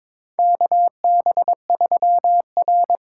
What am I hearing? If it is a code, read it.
Morse code, heard as a keyed tone, about 23 wpm, 700 Hz. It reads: KB3R
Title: USA Call Signs - 6 Characters - 6x Character Spacing - 22wpm